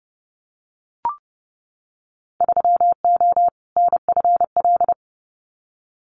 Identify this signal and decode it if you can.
Morse code, heard as a keyed tone, about 30 wpm, 700 Hz. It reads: 3O NFL
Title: Winter Field Day Exchanges 30wpm